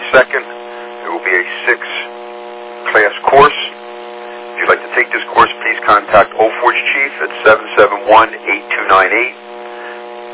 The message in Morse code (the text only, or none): none